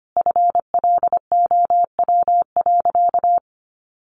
Morse code, heard as a keyed tone, about 25 wpm, 700 Hz. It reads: FLOW.
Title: Sentences from Top 1200 Words 25wpm